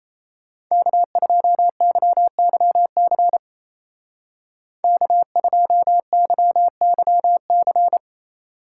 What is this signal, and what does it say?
Morse code, heard as a keyed tone, about 33 wpm, 700 Hz. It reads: K2YYC K2YYC
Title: Speed Racing - USA 1x3 Call Signs 28wpm